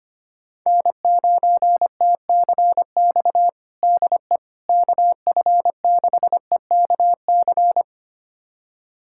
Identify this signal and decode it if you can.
Morse code, heard as a keyed tone, about 25 wpm, 700 Hz. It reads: N9TCX DE KF6EKC
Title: USA Callsign DE Callsign 25wpm